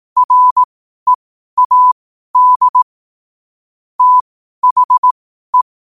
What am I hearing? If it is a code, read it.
Morse code, heard as a keyed tone, about 18 wpm, spaced as for 12 wpm, 1000 Hz. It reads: READ THE
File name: qotdfiles / 12wpm